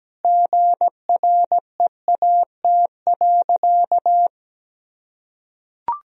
Morse code, heard as a keyed tone, about 17 wpm, 700 Hz. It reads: GREAT.
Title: Sentences from Top 1000 Words - Review Part 4 - 17wpm